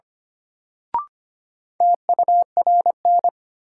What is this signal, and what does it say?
Morse code, heard as a keyed tone, about 25 wpm, 700 Hz. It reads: TURN
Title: Sentences from Top 200 Words - New Word Drill - 25wpm